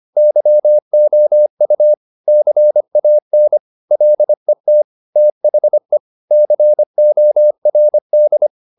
Morse code, heard as a keyed tone, about 25 wpm, 600 Hz. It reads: YOU CAN LET THE CORD
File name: the old man and the sea part 1 25wpm